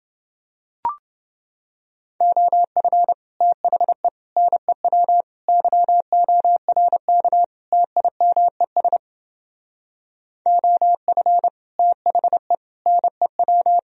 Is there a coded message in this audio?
Yes: Morse code from 2.2 to 13.8 s, heard as a keyed tone, about 30 wpm, 700 Hz. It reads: OF THE NEW YORK TIMES OF THE NEW